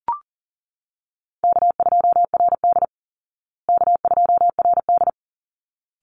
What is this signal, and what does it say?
Morse code, heard as a keyed tone, about 40 wpm, 700 Hz. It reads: K2RD K2RD
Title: Call Signs Repeated 3x - 40wpm